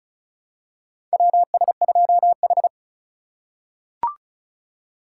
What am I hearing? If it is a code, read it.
Morse code, heard as a keyed tone, about 36 wpm, 700 Hz. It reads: WS2H
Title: USA Call Signs - 4 Characters - 3x Character Spacing - 35wpm